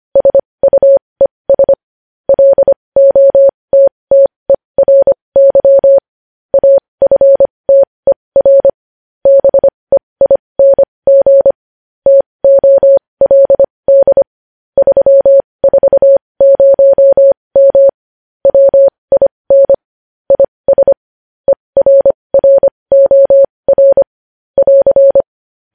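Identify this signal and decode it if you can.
Morse code, heard as a keyed tone, about 25 wpm, spaced as for 20 wpm, 550 Hz. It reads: SUES LOTTERY AFTER BEING TOLD 340M WIN IS ERROR <AR>
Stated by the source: Morse news headlines sent at 20.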